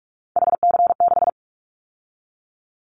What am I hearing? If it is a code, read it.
Morse code, heard as a keyed tone, about 44 wpm, 700 Hz. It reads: HC6
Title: Callsign Prefixes for Top 100 DXCC - 45wpm